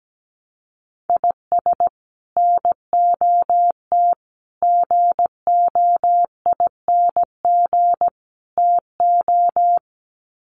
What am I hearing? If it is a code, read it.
Morse code, heard as a keyed tone, about 17 wpm, 700 Hz. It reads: IS NOT GOING TO
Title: Sets of 4 Words 17wpm